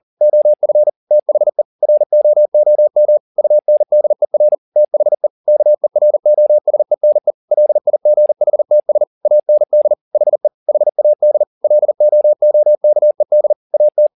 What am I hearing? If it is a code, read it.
Morse code, heard as a keyed tone, about 40 wpm, 600 Hz. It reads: OF THE ROOM UNDER THE KEROSENE LIGHTS AND HE HAD LOOKED AT
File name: the old man and the sea 2 40wpm